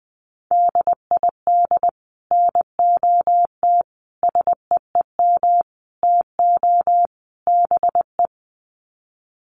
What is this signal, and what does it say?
Morse code, heard as a keyed tone, about 20 wpm, 700 Hz. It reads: DID NOT SEEM TO BE
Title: Sets of 5 Words Encore 20wpm